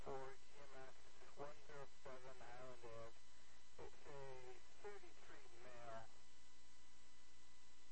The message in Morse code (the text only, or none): none